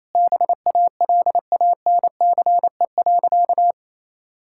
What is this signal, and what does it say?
Morse code, heard as a keyed tone, about 28 wpm, 700 Hz. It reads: BALANCE.
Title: Sentences from Top 1000 Words - Review Part 6 - 28wpm